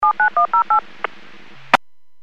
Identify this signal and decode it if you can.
DTMF, heard as telephone touch tones, tones of about 110 ms. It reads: 79108
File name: 451.25